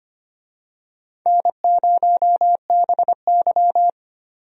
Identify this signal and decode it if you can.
Morse code, heard as a keyed tone, about 25 wpm, 700 Hz. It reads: N0BY
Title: USA Call Signs - 4 Characters - 25wpm